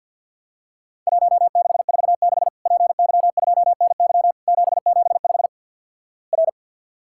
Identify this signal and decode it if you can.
Morse code, heard as a keyed tone, about 50 wpm, 700 Hz. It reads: 1646 PY2NY 775 R
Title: Worked All Europe DX Contest - QTC Exchange - 50wpm